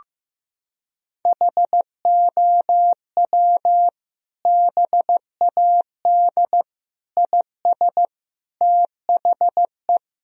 Morse code, heard as a keyed tone, about 15 wpm, 700 Hz. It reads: HOW BAD IS THE
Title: Sentences from Top 1000 Words - Review Part 3 - 15wpm